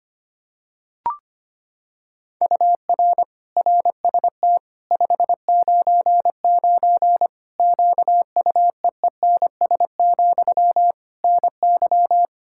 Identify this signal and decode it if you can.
Morse code, heard as a keyed tone, about 25 wpm, 700 Hz. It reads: UR RST 599 QUEENS, NY
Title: Practice QSO - Exchange 1 - Beginning - 25wpm